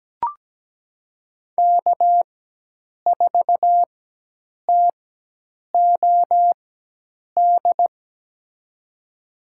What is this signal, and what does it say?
Morse code, heard as a keyed tone, about 17 wpm, spaced as for 8 wpm, 700 Hz. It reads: K4TOD
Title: USA Call Signs - 5 Characters - 4x Character Spacing - 17wpm